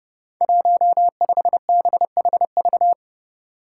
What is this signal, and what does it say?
Morse code, heard as a keyed tone, about 30 wpm, 700 Hz. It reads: 15BHV
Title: Five Challenging Letter-Number Combinations 30wpm